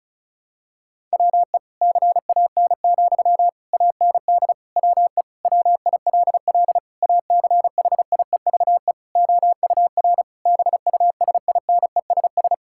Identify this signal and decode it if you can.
Morse code, heard as a keyed tone, about 36 wpm, 700 Hz. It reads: WE CAN, AND WE WILL ACHIEVE OUR BUSINESS
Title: Sentences from Top 1000 Words - Review Part 6 - 35wpm